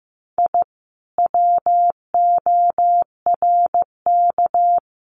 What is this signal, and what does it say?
Morse code, heard as a keyed tone, about 15 wpm, 700 Hz. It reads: I WORK
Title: Common QSO Phrases Encore 15wpm